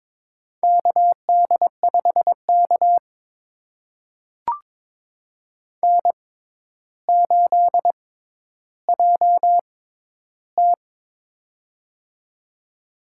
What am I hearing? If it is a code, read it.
Morse code, heard as a keyed tone, about 22 wpm, spaced as for 11 wpm, 700 Hz. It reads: KD5K N8JT
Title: USA Call Signs - 4 Characters - 6x Character Spacing - 22wpm